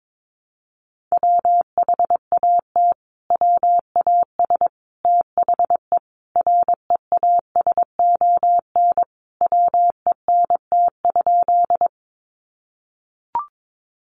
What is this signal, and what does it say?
Morse code, heard as a keyed tone, about 22 wpm, 700 Hz. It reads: WHAT WAS THE REASON WENT?
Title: Sentences from Top 700 Words 22wpm